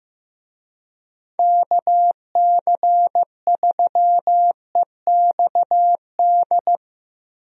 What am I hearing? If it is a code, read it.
Morse code, heard as a keyed tone, about 15 wpm, 700 Hz. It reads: KC3EXD
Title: USA Call Signs - 6 Characters - 15wpm